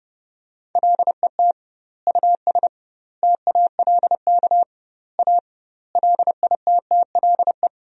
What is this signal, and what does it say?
Morse code, heard as a keyed tone, about 30 wpm, 700 Hz. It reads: LET US TALK A LITTLE
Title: 2x Word-Spacing - Sentences from Top 400 Words - 30wpm